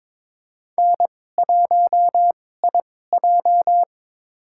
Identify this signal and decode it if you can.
Morse code, heard as a keyed tone, about 21 wpm, spaced as for 16 wpm, 700 Hz. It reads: N1IJ
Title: USA Call Signs - 4 Characters - 2x Character Spacing - 22wpm